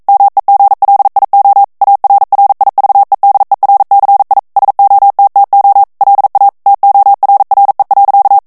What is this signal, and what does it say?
Morse code, heard as a keyed tone, about 42 wpm, 800 Hz. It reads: MEGLIO ARRIVEDERCI SOTTO LA TORRE.
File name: morse mp3 / ITALIEN texte clair 1 vitesse 40 mots minute